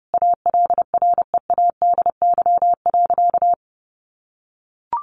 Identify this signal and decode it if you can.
Morse code, heard as a keyed tone, about 30 wpm, 700 Hz. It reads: ALREADY.
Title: Sentences from Top 900 Words 30wpm